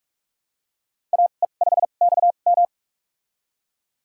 Morse code, heard as a keyed tone, about 42 wpm, spaced as for 31 wpm, 700 Hz. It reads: AE5XM
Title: USA Call Signs - 5 Characters - 2x Character Spacing - 45wpm